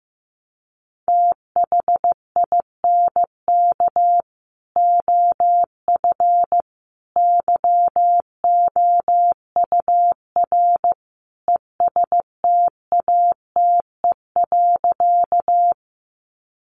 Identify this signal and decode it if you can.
Morse code, heard as a keyed tone, about 15 wpm, 700 Hz. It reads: THINK OF YOUR ESTATE.
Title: Sentences from Top 1200 Words 15wpm